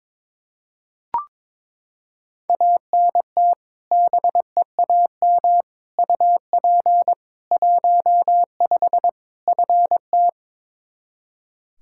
Morse code, heard as a keyed tone, about 22 wpm, 700 Hz. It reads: ANT BEAM UP 15 FT
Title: Practice QSO - Exchange 2 - Info - 22wpm